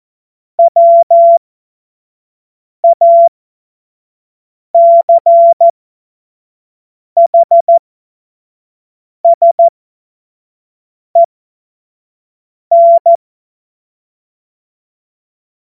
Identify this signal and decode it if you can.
Morse code, heard as a keyed tone, about 14 wpm, spaced as for 5 wpm, 680 Hz. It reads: WACHSEN
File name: Morsekurs / Witze / Witz 22. August 2023 25 F70